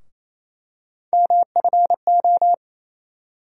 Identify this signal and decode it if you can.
Morse code, heard as a keyed tone, about 28 wpm, 700 Hz. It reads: MFO
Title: USA Call Sign Postfixes - 3 Letters 28wpm